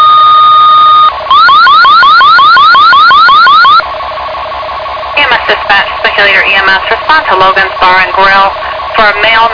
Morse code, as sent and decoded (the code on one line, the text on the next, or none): none